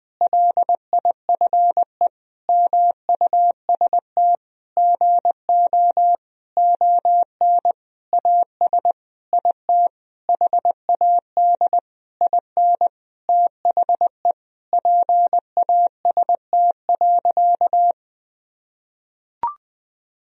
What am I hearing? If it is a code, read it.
Morse code, heard as a keyed tone, about 20 wpm, 700 Hz. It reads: LIFE MUST GO ON AS IT HAD IN THE PAST.
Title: Sentences from Top 1000 Words - Review Part 3 - 20wpm